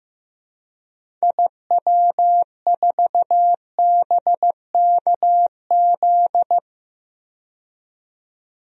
Morse code, heard as a keyed tone, about 15 wpm, 700 Hz. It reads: IW4BKZ
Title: DXCC-100 Call Signs - 6 Characters - 15wpm